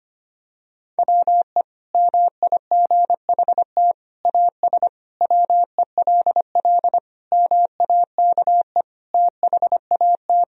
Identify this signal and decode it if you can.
Morse code, heard as a keyed tone, about 25 wpm, 700 Hz. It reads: WE MIGHT AS WELL MAKE THAT
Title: Sentences from Top 1000 Words - Review Part 1 - 25wpm